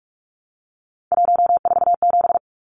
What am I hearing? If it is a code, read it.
Morse code, heard as a keyed tone, about 44 wpm, 700 Hz. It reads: 147
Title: Numbers - 3-Digit 45wpm